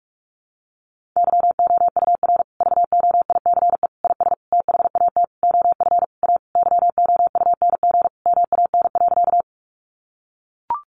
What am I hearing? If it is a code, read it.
Morse code, heard as a keyed tone, about 44 wpm, 700 Hz. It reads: YOUR VOICE IS THAT OF A YOUNG MAN.
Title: Sentences from Top 500 Words 45wpm